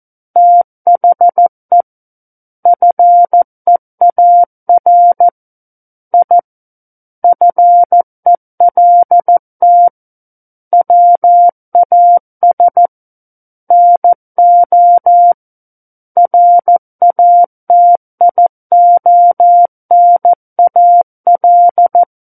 Morse code, heard as a keyed tone, about 14 wpm, 700 Hz. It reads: THE FEAR I FELT WAS NO RATIONAL
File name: Book One / 05-Chapter 5 - 14 WPM